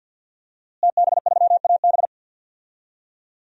Morse code, heard as a keyed tone, about 49 wpm, 700 Hz. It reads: TB3AB
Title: DXCC-100 Call Signs - 5 Characters - Repeated 3x - 50wpm